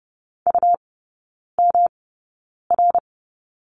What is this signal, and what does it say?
Morse code, heard as a keyed tone, about 30 wpm, spaced as for 9 wpm, 700 Hz. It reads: UMR
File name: Lesson 18e - Letter m - Three Characters 30wpm